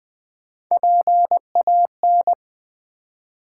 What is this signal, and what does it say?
Morse code, heard as a keyed tone, about 20 wpm, 700 Hz. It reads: PAN
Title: Build A Word - 1 Letter At A Time - 3-Letter Words - 20wpm